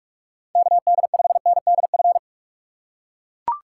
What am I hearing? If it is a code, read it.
Morse code, heard as a keyed tone, about 43 wpm, 700 Hz. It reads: KD5NDF
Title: USA Call Signs - 6 Characters - 6x Character Spacing - 45wpm